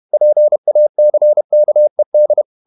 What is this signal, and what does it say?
Morse code, heard as a keyed tone, about 31 wpm, 600 Hz. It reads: PACKED
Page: Never 31wpm | CW med Gnister